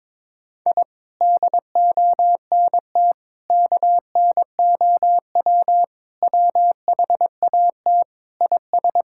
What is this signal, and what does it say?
Morse code, heard as a keyed tone, about 22 wpm, 700 Hz. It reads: I DONT KNOW WHAT IS
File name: Sentences with 7-Word Phrases - 22wpm